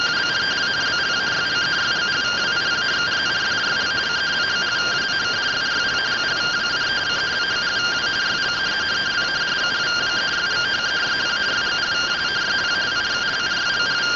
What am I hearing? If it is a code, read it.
Two-tone teleprinter signal, about 170 Hz shift.